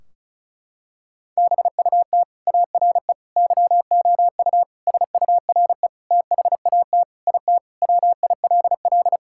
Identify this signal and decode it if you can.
Morse code, heard as a keyed tone, about 36 wpm, 700 Hz. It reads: BUT ARE YOU SURE THAT IT WILL
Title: Sentences from Top 1600 Words 35wpm